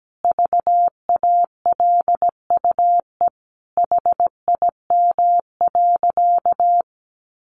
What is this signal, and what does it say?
Morse code, heard as a keyed tone, about 17 wpm, 700 Hz. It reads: VALUE HIM.
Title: Sentences from Top 600 Words 17wpm